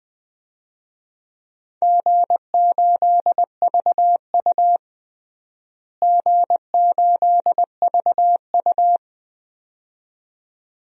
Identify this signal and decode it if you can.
Morse code, heard as a keyed tone, about 20 wpm, 700 Hz. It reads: G8VU G8VU
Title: DXCC-100 Call Signs - 4 Characters - Repeated 2x - 20wpm